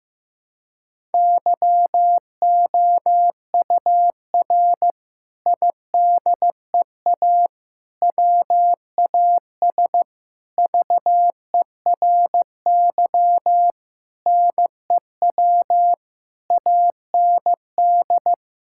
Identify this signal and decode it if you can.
Morse code, heard as a keyed tone, about 15 wpm, 700 Hz. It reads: YOUR IDEA WAS VERY NEW AND
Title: Sentences from Top 1000 Words - Review Part 6 - 15wpm